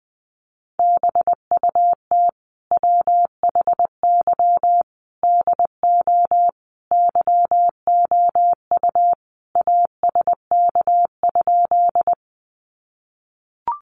Morse code, heard as a keyed tone, about 20 wpm, 700 Hz. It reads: BUT WHY DO YOU ASK?
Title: Sentences from Top 200 Words 20wpm